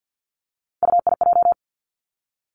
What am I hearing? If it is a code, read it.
Morse code, heard as a keyed tone, about 50 wpm, 700 Hz. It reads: UIJ
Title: Three Challenging Letter-Number Combinations 50wpm